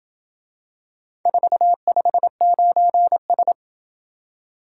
Morse code, heard as a keyed tone, about 27 wpm, 700 Hz. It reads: 459S